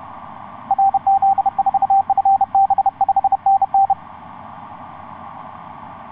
Morse code, heard as a keyed tone, about 30 wpm, 780 Hz. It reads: RZ4FB 5NN
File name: xv9dx ik1avw etc 12m cw 16feb2010 0909